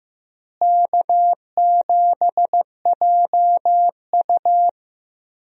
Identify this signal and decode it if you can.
Morse code, heard as a keyed tone, about 15 wpm, 700 Hz. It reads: K7JU
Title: USA Call Signs - 4 Characters - Repeated 2x - 15wpm